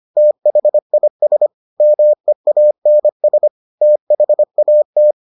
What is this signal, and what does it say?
Morse code, heard as a keyed tone, about 25 wpm, 600 Hz. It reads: THIS MEANS THAT